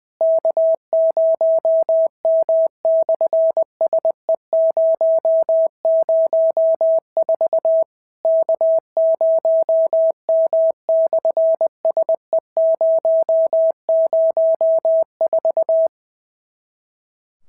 Morse code, heard as a keyed tone, about 20 wpm, 650 Hz. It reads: K0M/SE004 K0M/SE004
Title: SOTA QSO Practice 20wpm